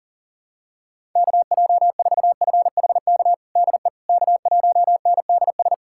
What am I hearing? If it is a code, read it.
Morse code, heard as a keyed tone, about 40 wpm, 700 Hz. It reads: KJ4FHK DE K1NDS
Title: USA Callsign DE Callsign 40wpm